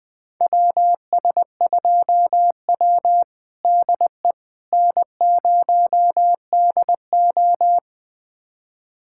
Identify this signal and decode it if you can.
Morse code, heard as a keyed tone, about 21 wpm, 700 Hz. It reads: WS2W DE N0DO
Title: USA Callsign DE Callsign - 4 Characters - 20wpm